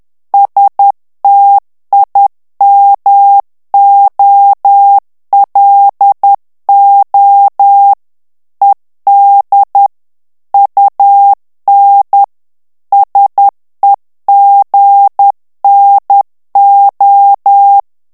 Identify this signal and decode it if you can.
Morse code, heard as a keyed tone, about 10 wpm, 800 Hz. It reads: STIMOLO ED UN SEGNO
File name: morse mp3 / ITALIEN texte clair 1 vitesse 10 mots minute